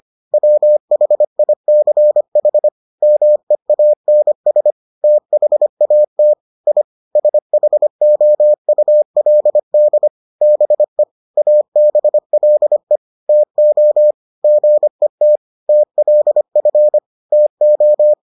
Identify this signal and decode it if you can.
Morse code, heard as a keyed tone, about 25 wpm, 600 Hz. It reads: WHICH MEANS THAT I SHOULD BE ABLE TO GET TLF TO